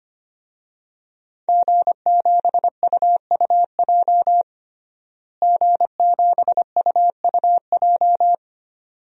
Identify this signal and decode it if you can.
Morse code, heard as a keyed tone, about 25 wpm, 700 Hz. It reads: G7UUJ G7UUJ
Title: DXCC-100 Call Signs - 5 Characters - Repeated 2x - 25wpm